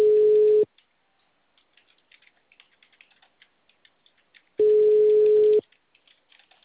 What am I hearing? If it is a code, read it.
Ringback tone (Europe, 425 Hz).